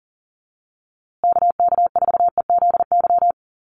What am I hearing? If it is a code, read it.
Morse code, heard as a keyed tone, about 40 wpm, 700 Hz. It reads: KK4EZY
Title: USA Call Signs Repeated 2x 40wpm